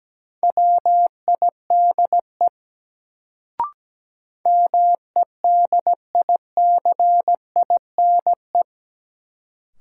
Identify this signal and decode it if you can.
Morse code, heard as a keyed tone, about 17 wpm, 700 Hz. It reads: WIDE MEDICINE
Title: Sentences from Top 900 Words - New Word Drill - 17wpm